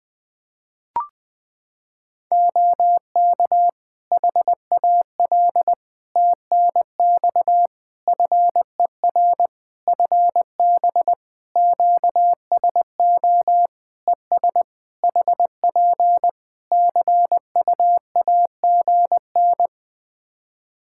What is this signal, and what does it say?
Morse code, heard as a keyed tone, about 21 wpm, 700 Hz. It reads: OK HAL TNX FER FB QSO ES HP CU AGN
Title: Practice QSO - Exchange 4 - Ending - 20wpm